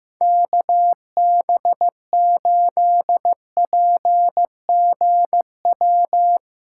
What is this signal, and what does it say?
Morse code, heard as a keyed tone, about 16 wpm, 700 Hz. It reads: KB8PGW
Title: K1USN SST QSO Practice - 15wpm